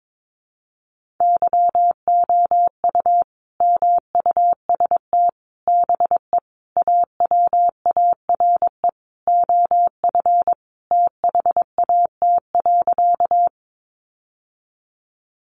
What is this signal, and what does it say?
Morse code, heard as a keyed tone, about 22 wpm, 700 Hz. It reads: YOU MUST BE AWARE OF THAT.
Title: Sentences from Top 1200 Words 22wpm